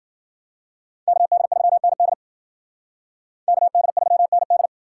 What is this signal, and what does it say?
Morse code, heard as a keyed tone, about 51 wpm, 700 Hz. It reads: KD3ND KD3ND